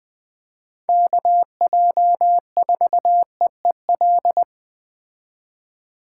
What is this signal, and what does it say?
Morse code, heard as a keyed tone, about 20 wpm, 700 Hz. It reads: KJ4EEL